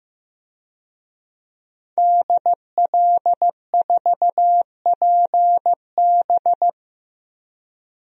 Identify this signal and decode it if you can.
Morse code, heard as a keyed tone, about 15 wpm, 700 Hz. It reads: DL4PB
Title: DXCC-100 Call Signs - 5 Characters - Repeated 2x - 15wpm